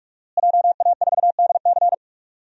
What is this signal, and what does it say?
Morse code, heard as a keyed tone, about 43 wpm, 700 Hz. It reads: JA4DC
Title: DXCC-100 Call Signs - 5 Characters - Repeated 2x - 45wpm